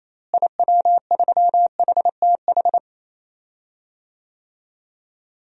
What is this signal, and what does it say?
Morse code, heard as a keyed tone, about 28 wpm, 700 Hz. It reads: IW3HTH
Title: DXCC-100 Call Signs - 6 Characters - Repeated 3x - 28wpm